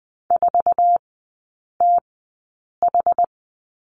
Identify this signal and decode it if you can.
Morse code, heard as a keyed tone, about 20 wpm, spaced as for 8 wpm, 700 Hz. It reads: 4TH
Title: 133 Lesson 22e - Question Mark - Three Characters 20wpm